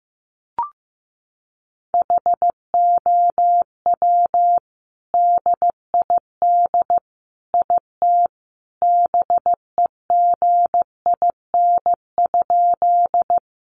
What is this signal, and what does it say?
Morse code, heard as a keyed tone, about 15 wpm, 700 Hz. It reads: HOW DID IT BEGIN?
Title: Sentences from Top 300 Words 15wpm